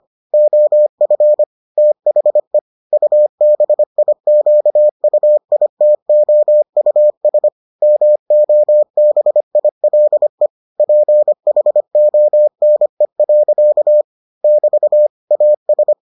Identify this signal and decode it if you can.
Morse code, heard as a keyed tone, about 25 wpm, 600 Hz. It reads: OF THE UBIQUITOUS MOBILE PHONE. <BT> AS